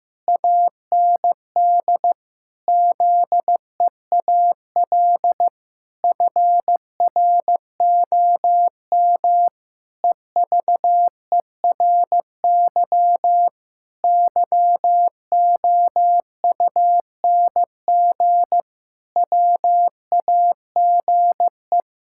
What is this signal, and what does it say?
Morse code, heard as a keyed tone, about 15 wpm, 700 Hz. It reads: AND ZEAL FROM EVERY YOUNG WAGE
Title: Pangram Challenge v2 - 15wpm